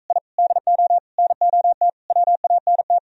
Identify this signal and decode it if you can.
Morse code, heard as a keyed tone, about 45 wpm, 700 Hz. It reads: I DO NOT WANT